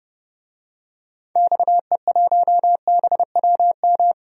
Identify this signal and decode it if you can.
Morse code, heard as a keyed tone, about 30 wpm, 700 Hz. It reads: XE1BWM
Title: DXCC-100 Call Signs - 6 Characters - 30wpm